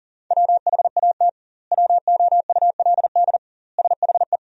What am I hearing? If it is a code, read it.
Morse code, heard as a keyed tone, about 37 wpm, 700 Hz. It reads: WHAT WOULD SHE
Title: 2x Word-Spacing - Sentences from Top 100 Words - 40wpm